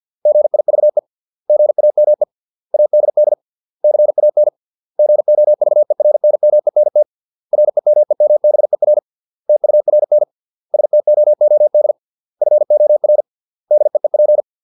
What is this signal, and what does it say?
Morse code, heard as a keyed tone, about 50 wpm, 600 Hz. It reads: GIVE GAME ADD CAN GOVERNMENT REMEMBER TURN STOOD FOR DEEP